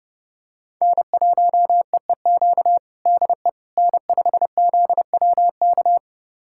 Morse code, heard as a keyed tone, about 30 wpm, 700 Hz. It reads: N1EEQ DE N5ZWK
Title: USA Callsign DE Callsign - 5 Characters - 30wpm